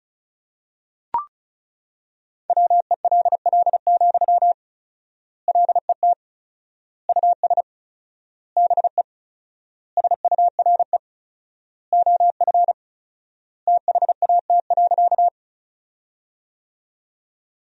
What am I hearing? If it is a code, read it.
Morse code, heard as a keyed tone, about 34 wpm, 700 Hz. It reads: WELL, LET US BE SURE OF THAT.
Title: 4x Word-Spacing - Sentences from Top 300 Words - 35wpm